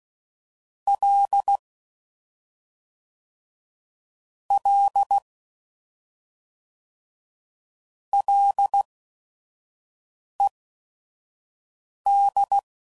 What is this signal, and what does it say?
Morse code, heard as a keyed tone, about 16 wpm, spaced as for 5 wpm, 800 Hz. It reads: L L LED
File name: Prefixes/FLs & THs/Single L's